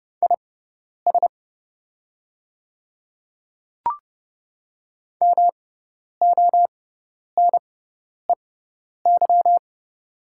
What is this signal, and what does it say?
Morse code, heard as a keyed tone, about 30 wpm, spaced as for 10 wpm, 690 Hz. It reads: IS MONEY
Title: 6x Character Spacing - Top 500 Words - 30wpm